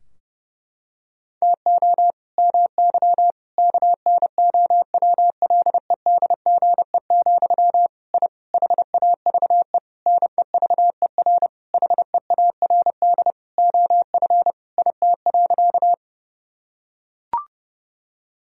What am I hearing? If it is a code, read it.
Morse code, heard as a keyed tone, about 30 wpm, 700 Hz. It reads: TO MY KNOWLEDGE, I HAVE NEVER HEARD OF IT.
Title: Sentences from Top 1000 Words - Review Part 3 - 30wpm